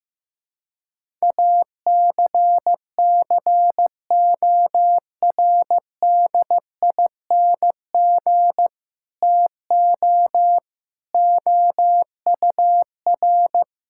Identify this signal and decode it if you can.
Morse code, heard as a keyed tone, about 15 wpm, 700 Hz. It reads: ACCORDING TO OUR
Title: Sentences from Top 1000 Words - Review Part 5 - 15wpm